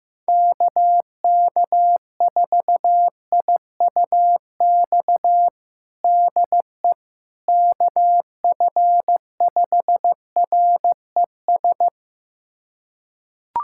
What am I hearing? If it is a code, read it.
Morse code, heard as a keyed tone, about 15 wpm, 700 Hz. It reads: KK4IUX DE KF5RES
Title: USA Callsign DE Callsign 15wpm